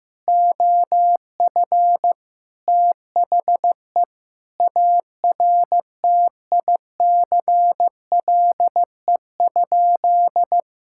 Morse code, heard as a keyed tone, about 15 wpm, 700 Hz. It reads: OF THE ARTICLE?
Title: Sentences from Top 1000 Words - Review Part 5 - 15wpm